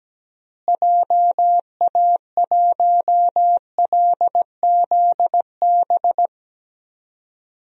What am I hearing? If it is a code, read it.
Morse code, heard as a keyed tone, about 17 wpm, 700 Hz. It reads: JA1LZB
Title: DXCC-100 Call Signs - 6 Characters - Repeated 2x - 17wpm